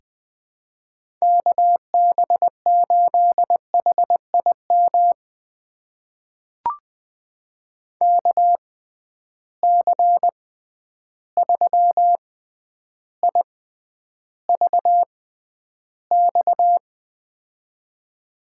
Morse code, heard as a keyed tone, about 20 wpm, spaced as for 10 wpm, 700 Hz. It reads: KB8HIM KC3IVX
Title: USA Call Signs - 6 Characters - 6x Character Spacing - 20wpm